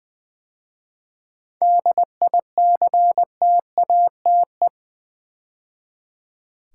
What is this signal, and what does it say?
Morse code, heard as a keyed tone, about 20 wpm, 700 Hz. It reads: DICTATE
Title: 7 Letter Words - First 3 Letters Given - 20wpm